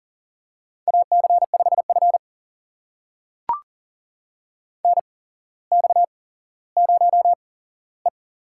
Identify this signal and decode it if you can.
Morse code, heard as a keyed tone, about 40 wpm, spaced as for 17 wpm, 700 Hz. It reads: AC5F NX0E
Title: USA Call Signs - 4 Characters - 8x Character Spacing - 40wpm